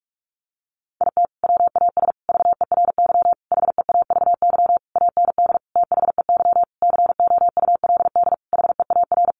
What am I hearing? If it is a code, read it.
Morse code, heard as a keyed tone, about 44 wpm, 700 Hz. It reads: IT WAS VERY HEAVY AND THEY COULD HEAR
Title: Sentences from Top 1100 Words 45wpm